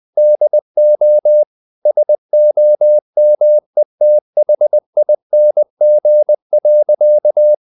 Morse code, heard as a keyed tone, about 20 wpm, 600 Hz. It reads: DO SOMETHING.
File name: the old man and the sea / the old man and the sea part 1 20wpm